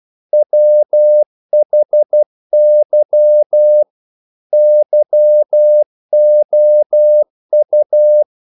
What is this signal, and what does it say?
Morse code, heard as a keyed tone, about 12 wpm, 600 Hz. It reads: WHY YOU
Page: English quotations in CW (telegraphy - morse)